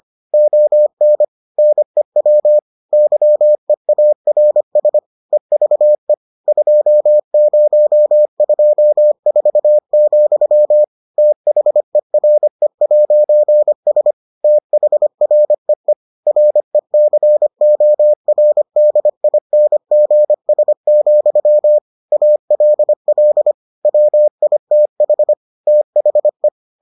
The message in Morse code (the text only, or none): ON NEW YEARS EVE 2024, THERE'S THREE RECORDINGS, ALL WITH THE